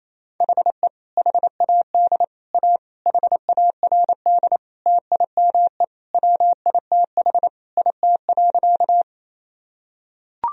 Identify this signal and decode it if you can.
Morse code, heard as a keyed tone, about 28 wpm, 700 Hz. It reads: HE HAD A HARD TIME WITH IT.
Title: Sentences from Top 1000 Words - Review Part 1 - 28wpm